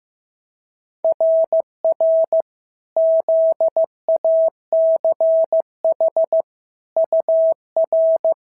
Morse code, heard as a keyed tone, about 14 wpm, 650 Hz. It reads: RR ZACH UR
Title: SOTA QSO Practice 15wpm